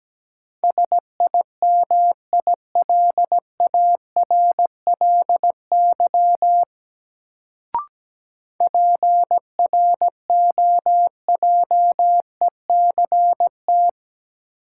Morse code, heard as a keyed tone, about 17 wpm, 700 Hz. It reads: SIMILARLY PROJECT
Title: Sentences from Top 1400 Words - New Word Drill Alt - 17wpm